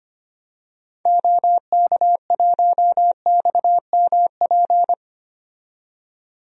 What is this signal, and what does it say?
Morse code, heard as a keyed tone, about 25 wpm, 700 Hz. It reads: OK1XMP
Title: DXCC-100 Call Signs - 6 Characters - 25wpm